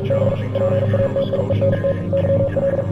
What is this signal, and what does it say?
Morse code, heard as a keyed tone, about 46 wpm, 550 Hz. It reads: B PATIENT YL